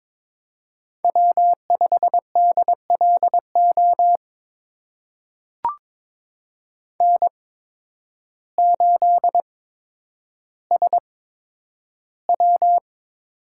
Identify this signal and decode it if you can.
Morse code, heard as a keyed tone, about 22 wpm, spaced as for 10 wpm, 700 Hz. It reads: W5DLO N8SW
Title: USA Call Signs - 5 Characters - 8x Character Spacing - 22wpm